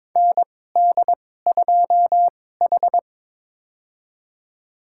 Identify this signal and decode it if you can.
Morse code, heard as a keyed tone, about 22 wpm, spaced as for 16 wpm, 700 Hz. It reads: ND2H
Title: USA Call Signs - 4 Characters - 2x Character Spacing - 22wpm